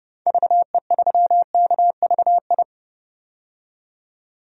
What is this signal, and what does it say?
Morse code, heard as a keyed tone, about 30 wpm, 700 Hz. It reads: VE3KVI
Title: DXCC-100 Call Signs - 6 Characters - 30wpm